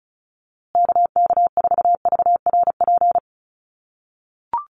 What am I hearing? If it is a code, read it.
Morse code, heard as a keyed tone, about 34 wpm, 700 Hz. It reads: KK4VRP
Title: USA Call Signs Repeated 3x 35wpm